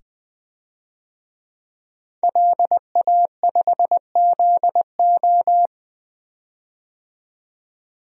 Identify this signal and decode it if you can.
Morse code, heard as a keyed tone, about 20 wpm, 700 Hz. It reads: LA5ZO
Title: DXCC-100 Call Signs - 5 Characters - 20wpm